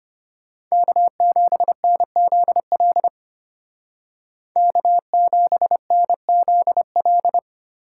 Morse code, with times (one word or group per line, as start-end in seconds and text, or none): 0.7-3.1 K7NZL
4.6-7.4 K7NZL